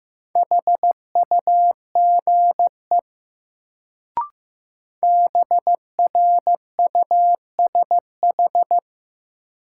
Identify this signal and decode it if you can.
Morse code, heard as a keyed tone, about 15 wpm, 700 Hz. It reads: HUGE BRUSH
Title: Sentences from Top 1100 Words - New Word Drill Alt - 15wpm